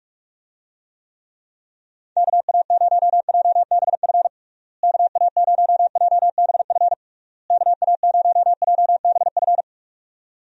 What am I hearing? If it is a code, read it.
Morse code, heard as a keyed tone, about 41 wpm, 700 Hz. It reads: KA0JBF KA0JBF KA0JBF
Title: USA Call Signs - 6 Characters - Repeated 3x - 45wpm